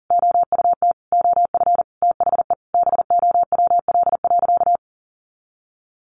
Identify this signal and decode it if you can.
Morse code, heard as a keyed tone, about 40 wpm, 700 Hz. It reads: OUT OF THE BOWL.
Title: Sentences from Top 1200 Words 40wpm